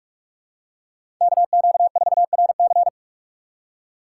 Morse code, heard as a keyed tone, about 46 wpm, 690 Hz. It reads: KQ4RC
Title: USA Call Signs - 5 Characters - 6x Character Spacing - 45wpm